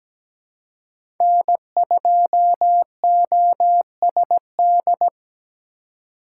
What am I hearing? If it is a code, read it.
Morse code, heard as a keyed tone, about 17 wpm, 700 Hz. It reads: N2OSD
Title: USA Call Signs - 5 Characters - 2x Character Spacing - 17wpm